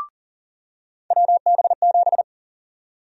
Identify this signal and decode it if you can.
Morse code, heard as a keyed tone, about 40 wpm, 700 Hz. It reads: WB7
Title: USA Call Sign Prefixes - 40wpm